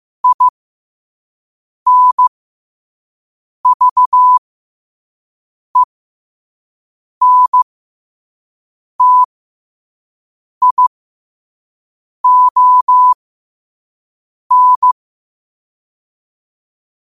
Morse code, heard as a keyed tone, about 15 wpm, spaced as for 5 wpm, 1000 Hz. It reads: INVENTION
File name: qotdfiles / 5wpm